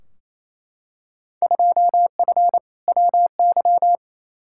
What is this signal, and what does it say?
Morse code, heard as a keyed tone, about 28 wpm, 700 Hz. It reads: 2F WY
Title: Field Day Exchanges Encore - 28wpm